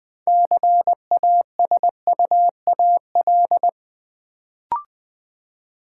Morse code, heard as a keyed tone, about 20 wpm, 700 Hz. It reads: CASUAL
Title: Build A Word - 3 Letters At A Time - 6-Letter Words - 20wpm